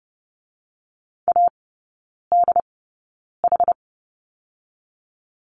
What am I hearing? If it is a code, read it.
Morse code, heard as a keyed tone, about 30 wpm, spaced as for 9 wpm, 700 Hz. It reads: ADH
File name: 081 Lesson 14e - Number 2 - Three Characters 30wpm